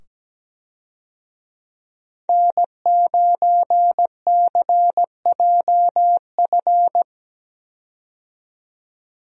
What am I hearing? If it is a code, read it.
Morse code, heard as a keyed tone, about 17 wpm, 700 Hz. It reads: N9CJF
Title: USA Call Signs - 5 Characters - 17wpm